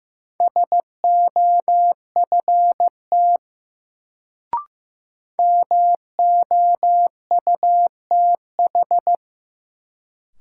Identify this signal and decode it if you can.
Morse code, heard as a keyed tone, about 15 wpm, 700 Hz. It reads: SOFT MOUTH
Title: Sentences from Top 1500 Words - New Word Drill Alt - 15wpm